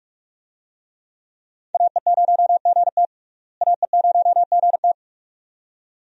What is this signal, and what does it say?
Morse code, heard as a keyed tone, about 41 wpm, 700 Hz. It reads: AE0GT AE0GT
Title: USA Call Signs - 5 Characters - Repeated 2x - 45wpm